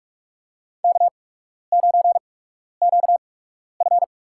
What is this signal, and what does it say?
Morse code, heard as a keyed tone, about 42 wpm, spaced as for 12 wpm, 700 Hz. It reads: K9QF
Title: USA Call Signs - 4 Characters - 8x Character Spacing - 45wpm